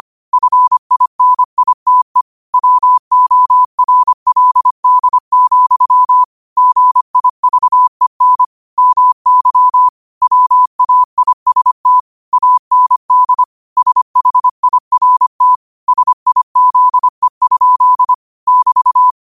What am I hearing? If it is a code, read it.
Morse code, heard as a keyed tone, about 25 wpm, 1000 Hz. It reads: FINITE WORLD, GIVEN MY WAIST AND SHIRT SIZE? <BT>
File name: qotdfiles / 25wpm